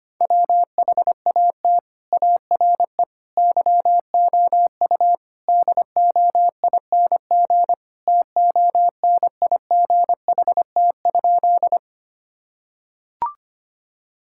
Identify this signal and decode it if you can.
Morse code, heard as a keyed tone, about 25 wpm, 700 Hz. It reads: WHAT ARE YOU DOING TONIGHT?
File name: practice / 25wpm / Sentences with 4-Word Phrases - 25wpm